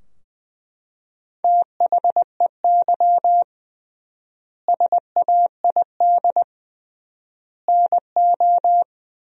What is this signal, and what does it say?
Morse code, heard as a keyed tone, about 19 wpm, 700 Hz. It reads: THEY SAID NO
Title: 3x Word-Spacing - Sentences from Top 100 Words - 20wpm